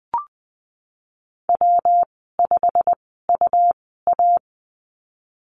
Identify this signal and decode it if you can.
Morse code, heard as a keyed tone, about 19 wpm, spaced as for 14 wpm, 700 Hz. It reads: W5UA
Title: 147 Lesson 24g - Letter y - Call Signs 20wpm